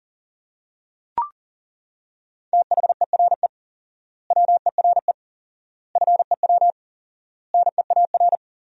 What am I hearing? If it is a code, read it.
Morse code, heard as a keyed tone, about 40 wpm, 700 Hz. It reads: THERE WERE FEW NEAR
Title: 4x Word-Spacing - Sentences from Top 300 Words - 40wpm